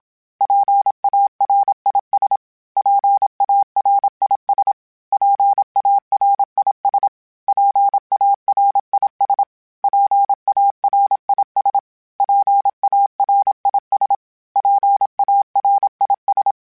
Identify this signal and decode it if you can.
Morse code, heard as a keyed tone, about 27 wpm, 800 Hz. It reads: PARIS PARIS PARIS PARIS PARIS PARIS PARIS
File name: PARIS Calibration / PARIS mp3s / PARIS@25